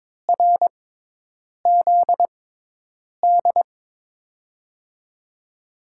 Morse code, heard as a keyed tone, about 22 wpm, spaced as for 8 wpm, 700 Hz. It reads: RZD
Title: USA Call Signs - 5 Characters - 6x Character Spacing - 22wpm